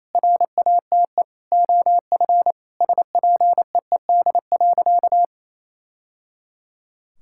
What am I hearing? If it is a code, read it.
Morse code, heard as a keyed tone, about 28 wpm, 700 Hz. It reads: RATE OF SPEED.
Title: Sentences from Top 1000 Words - Review Part 5 - 28wpm